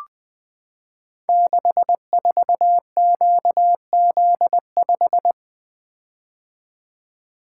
Morse code, heard as a keyed tone, about 20 wpm, 700 Hz. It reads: 64QZ5
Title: Code Groups - 5 Letter:Numbers - 20wpm